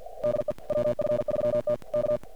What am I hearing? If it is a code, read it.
Morse code, heard as a keyed tone, about 53 wpm, 600 Hz, with noise. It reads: DE WF3T K
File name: archived / CRSnet